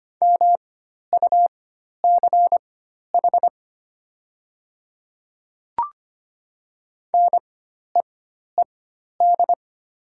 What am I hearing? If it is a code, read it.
Morse code, heard as a keyed tone, about 25 wpm, spaced as for 12 wpm, 700 Hz. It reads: MUCH NEED
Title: 4x Character Spacing - Top 200 Words - 25wpm